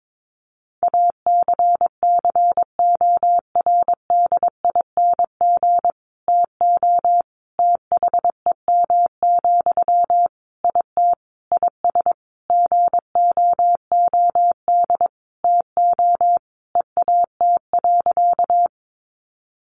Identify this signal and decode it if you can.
Morse code, heard as a keyed tone, about 22 wpm, 700 Hz. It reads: ACCORDING TO THEM, IT IS GOOD TO EAT.
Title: Sentences from Top 800 Words 22wpm